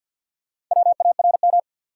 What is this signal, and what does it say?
Morse code, heard as a keyed tone, about 52 wpm, 700 Hz. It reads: WARM
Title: Sentences from Top 500 Words - New Word Drill - 50wpm